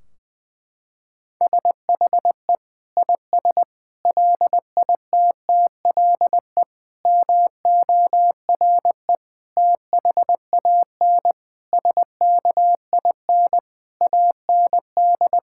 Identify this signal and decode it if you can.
Morse code, heard as a keyed tone, about 20 wpm, 700 Hz. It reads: SHE IS LITTLE MORE THAN SKIN AND
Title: Sentences from Top 1600 Words 20wpm